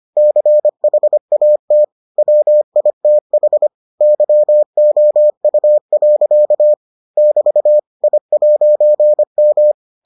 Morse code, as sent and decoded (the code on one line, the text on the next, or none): -.-. .... .- - / .-- .. - .... / -.-- --- ..- .-.-.- / -...- / .. .----. --
CHAT WITH YOU. <BT> I'M